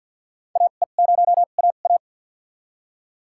Morse code, heard as a keyed tone, about 47 wpm, spaced as for 35 wpm, 700 Hz. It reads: AE0AA
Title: USA Call Signs - 5 Characters - 2x Character Spacing - 50wpm